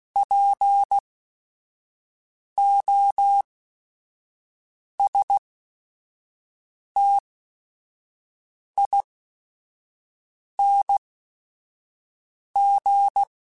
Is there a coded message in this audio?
Yes: Morse code from 0.2 to 13.2 s, heard as a keyed tone, about 16 wpm, spaced as for 5 wpm, 800 Hz. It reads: POSTING